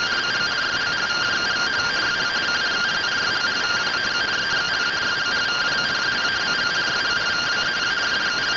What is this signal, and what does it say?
Two-tone teleprinter signal, about 170 Hz shift.
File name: findings / CT AXIAL -572mm 2.gen-33 rtty rx